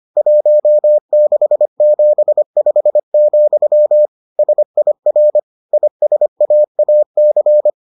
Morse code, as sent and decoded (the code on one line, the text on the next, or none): .---- -.... --... ..... --..-- / ... .. .-. / .. ... .- .- -.-.
1675, SIR ISAAC